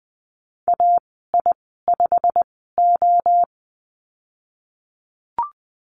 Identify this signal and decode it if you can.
Morse code, heard as a keyed tone, about 20 wpm, spaced as for 15 wpm, 700 Hz. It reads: AI5O
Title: 147 Lesson 24g - Letter y - Call Signs 20wpm